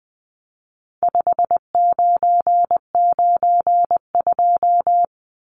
Morse code, heard as a keyed tone, about 20 wpm, 700 Hz. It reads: 5992
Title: Numbers - 4-Digit 20wpm